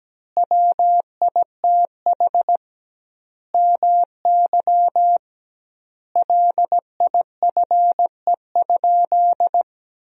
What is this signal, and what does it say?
Morse code, heard as a keyed tone, about 17 wpm, 700 Hz. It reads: WITH MY LIFE?
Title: 2x Word-Spacing - Sentences from Top 300 Words - 17wpm